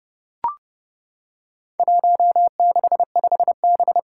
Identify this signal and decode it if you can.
Morse code, heard as a keyed tone, about 30 wpm, 700 Hz. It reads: 165B
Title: Four Challenging Letter-Number Combinations 30wpm